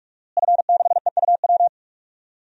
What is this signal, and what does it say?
Morse code, heard as a keyed tone, about 42 wpm, 700 Hz. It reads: F6EUW
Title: DXCC-100 Call Signs - 5 Characters - Repeated 2x - 45wpm